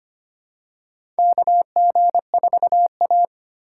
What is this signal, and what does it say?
Morse code, heard as a keyed tone, about 25 wpm, 700 Hz. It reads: KG4A
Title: USA Call Signs - 4 Characters - 8x Character Spacing - 25wpm